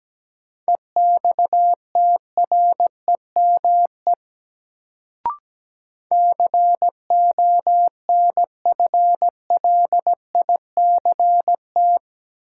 Morse code, heard as a keyed tone, about 17 wpm, 700 Hz. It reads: EXTREME CONFLICT
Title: Sentences from Top 1300 Words - New Word Drill Alt - 17wpm